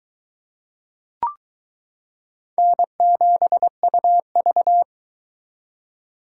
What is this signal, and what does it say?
Morse code, heard as a keyed tone, about 23 wpm, 700 Hz. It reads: N7UV